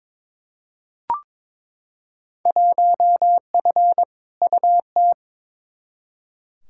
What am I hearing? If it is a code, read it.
Morse code, heard as a keyed tone, about 21 wpm, 700 Hz. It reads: 1F UT
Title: Field Day Exchanges Encore - 22wpm